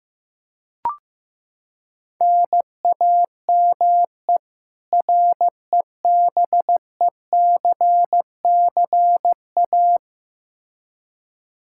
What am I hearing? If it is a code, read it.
Morse code, heard as a keyed tone, about 15 wpm, 700 Hz. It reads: NAME REBECCA
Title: K1USN SST QSO Practice 15wpm